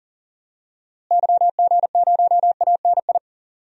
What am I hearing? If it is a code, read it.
Morse code, heard as a keyed tone, about 40 wpm, 700 Hz. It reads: YG0ANI
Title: DXCC-100 Call Signs - 6 Characters - Repeated 3x - 40wpm